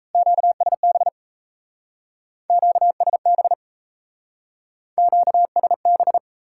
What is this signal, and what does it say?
Morse code, heard as a keyed tone, about 36 wpm, 700 Hz. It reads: QSB QSB QSB